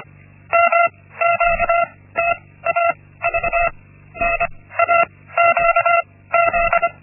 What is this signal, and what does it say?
Morse code, heard as a keyed tone, about 25 wpm, 680 Hz. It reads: M Q T A V N A Q Z
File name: Linksmakalnis-1